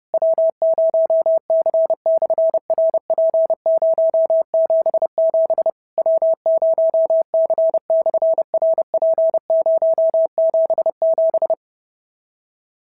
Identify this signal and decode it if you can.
Morse code, heard as a keyed tone, about 30 wpm, 650 Hz. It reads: W0C/RP077 W0C/RP077
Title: SOTA QSO Practice 30wpm